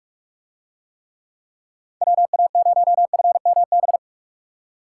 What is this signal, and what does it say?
Morse code, heard as a keyed tone, about 44 wpm, 700 Hz. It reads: WA0FMB